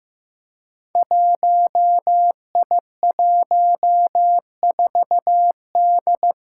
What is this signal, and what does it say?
Morse code, heard as a keyed tone, about 15 wpm, 700 Hz. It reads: 1I14D
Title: Five Challenging Letter-Number Combinations 15wpm